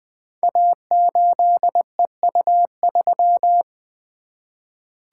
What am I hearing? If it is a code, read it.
Morse code, heard as a keyed tone, about 20 wpm, 700 Hz. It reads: A8EU3
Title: Code Groups - 5 Letter:Numbers - 20wpm